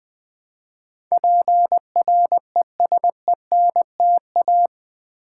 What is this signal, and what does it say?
Morse code, heard as a keyed tone, about 20 wpm, 700 Hz. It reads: PRESENTA
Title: Build A Word - 4 Letters At A Time - 12-Letter Words - 20wpm